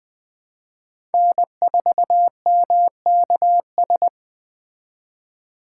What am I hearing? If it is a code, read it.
Morse code, heard as a keyed tone, about 20 wpm, 700 Hz. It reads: N4MKS
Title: Speed Racing - USA 1x3 Call Signs 17wpm